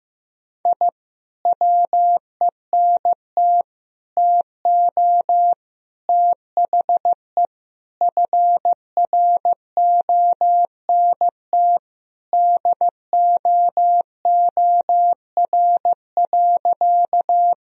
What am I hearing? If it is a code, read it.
Morse code, heard as a keyed tone, about 15 wpm, 700 Hz. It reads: I WENT TO THE FRONT DOOR.
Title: Sentences from Top 1000 Words - Review Part 2 - 15wpm